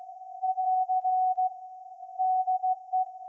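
Morse code, heard as a keyed tone, about 16 wpm, 740 Hz. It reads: <AR> DE
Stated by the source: Morse code from SAQ, received at a station in Putignano, Apulia, Italy, on VLF.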